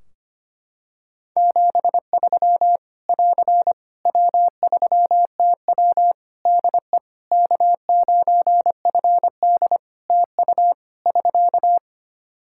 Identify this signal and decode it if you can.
Morse code, heard as a keyed tone, about 25 wpm, 700 Hz. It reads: 73 <AR> W3TW DE K9FD TU <SK>
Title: Practice QSO - Exchange 4 - Ending - 25wpm